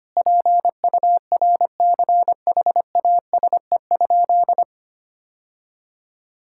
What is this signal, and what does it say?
Morse code, heard as a keyed tone, about 25 wpm, 700 Hz. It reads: PURCHASE?
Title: Sentences from Top 1000 Words - Review Part 6 - 25wpm